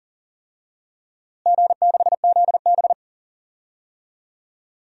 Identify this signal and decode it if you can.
Morse code, heard as a keyed tone, about 40 wpm, 700 Hz. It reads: G6ZB
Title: DXCC-100 Call Signs - 4 Characters - Repeated 3x - 40wpm